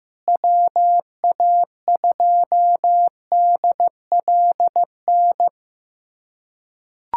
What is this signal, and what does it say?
Morse code, heard as a keyed tone, about 16 wpm, 700 Hz. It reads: WA2DLN
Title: USA Callsign DE Callsign - 6 Characters - 15wpm